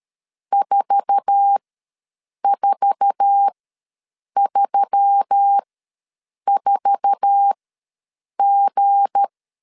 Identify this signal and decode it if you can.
Morse code, heard as a keyed tone, about 13 wpm, spaced as for 7 wpm, 800 Hz. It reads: 4434G
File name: LEZIONI / 8 YJ45 / 8 YJ4 ws 12wpm 3249